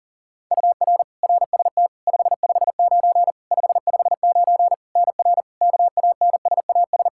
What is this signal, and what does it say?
Morse code, heard as a keyed tone, about 42 wpm, 700 Hz. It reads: UR RST 559 559 NR KANSAS
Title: K1USN SST QSO Practice 40wpm